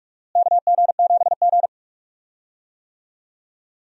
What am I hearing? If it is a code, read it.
Morse code, heard as a keyed tone, about 46 wpm, 700 Hz. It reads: KG7G
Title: USA Call Signs - 4 Characters - Repeated 2x - 45wpm